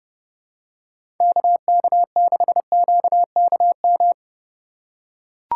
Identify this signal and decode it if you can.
Morse code, heard as a keyed tone, about 30 wpm, 700 Hz. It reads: KK6QKM
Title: USA Call Signs - 6 Characters - 2x Character Spacing - 30wpm